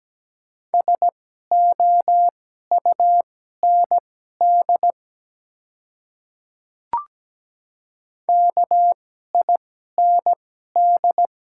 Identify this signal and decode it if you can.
Morse code, heard as a keyed tone, about 17 wpm, spaced as for 12 wpm, 700 Hz. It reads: SOUND KIND
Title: 2x Character Spacing - Top 200 Words - 17wpm